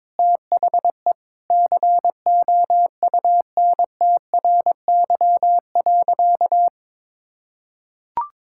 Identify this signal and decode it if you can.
Morse code, heard as a keyed tone, about 22 wpm, 700 Hz. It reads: THE COUNTRY.
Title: Sentences from Top 1000 Words - Review Part 6 - 22wpm